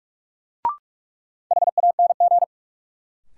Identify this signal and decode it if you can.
Morse code, heard as a keyed tone, about 47 wpm, 700 Hz. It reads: HANG
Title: Sentences from Top 1600 Words - New Word Drill - 45wpm